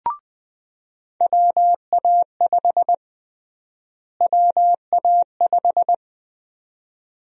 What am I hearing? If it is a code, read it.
Morse code, heard as a keyed tone, about 21 wpm, 700 Hz. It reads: WA5 WA5
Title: USA Call Sign Prefixes - Repeated 2x - 20wpm